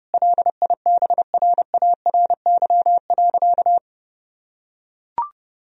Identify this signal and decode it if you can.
Morse code, heard as a keyed tone, about 32 wpm, 700 Hz. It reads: LIBRARY.
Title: Sentences from Top 1000 Words - Review Part 4 - 30wpm